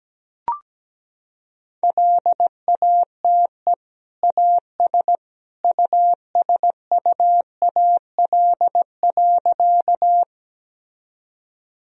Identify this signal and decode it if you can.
Morse code, heard as a keyed tone, about 17 wpm, 700 Hz. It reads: LATE AS USUAL.
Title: Sentences from Top 1000 Words - Review Part 2 - 17wpm